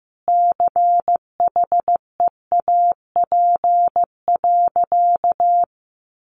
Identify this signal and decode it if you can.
Morse code, heard as a keyed tone, about 15 wpm, 700 Hz. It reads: CHEAP.
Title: Sentences from Top 1200 Words 15wpm